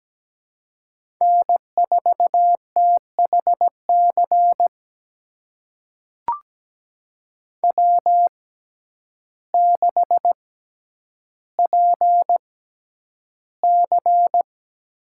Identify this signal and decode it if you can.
Morse code, heard as a keyed tone, about 17 wpm, spaced as for 9 wpm, 700 Hz. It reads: N4THC W6PC
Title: USA Call Signs - 5 Characters - 6x Character Spacing - 17wpm